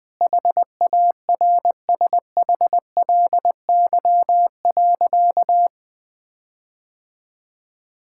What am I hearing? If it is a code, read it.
Morse code, heard as a keyed tone, about 21 wpm, 700 Hz. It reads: HARSHLY.
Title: Sentences with 6-Word Phrases - Spoken Beginning - 20wpm